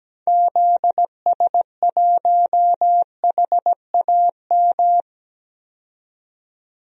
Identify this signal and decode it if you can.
Morse code, heard as a keyed tone, about 17 wpm, 700 Hz. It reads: ZS1HAM
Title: DXCC-100 Call Signs - 6 Characters - 17wpm